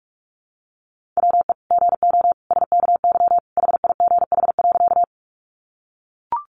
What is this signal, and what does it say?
Morse code, heard as a keyed tone, about 44 wpm, 700 Hz. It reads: WE GO SKY HIGH.
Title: Sentences from Top 1200 Words 45wpm